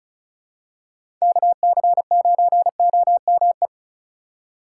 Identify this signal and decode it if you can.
Morse code, heard as a keyed tone, about 36 wpm, 700 Hz. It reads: KC9OME
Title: USA Call Signs - 6 Characters - 3x Character Spacing - 35wpm